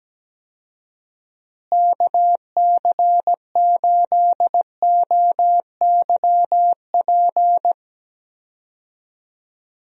Morse code, heard as a keyed tone, about 17 wpm, 700 Hz. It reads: KC8OYP
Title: USA Call Signs - 6 Characters - Repeated 2x - 17wpm